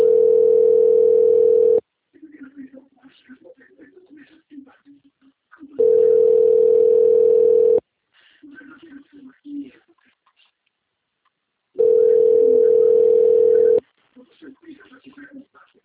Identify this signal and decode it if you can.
Ringback tone (North America), 440 + 480 Hz, with speech.
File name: out- - - - - .19518